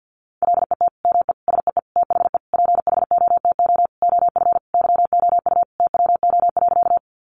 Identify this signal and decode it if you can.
Morse code, heard as a keyed tone, about 50 wpm, 700 Hz. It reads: LET ME SEE THE PHOTO OF YOU TWO.
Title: Sentences from Top 1200 Words 50wpm